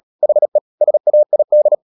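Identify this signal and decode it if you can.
Morse code, heard as a keyed tone, about 37 wpm, 600 Hz. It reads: HE SAID
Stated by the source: Morse code practice at 35–39 wpm sending English text of Ken Follett's novel Never, from CW med Gnister.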